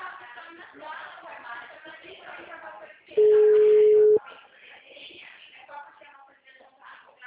Ringback tone (Europe, 425 Hz).
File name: out- -2010- - - .34382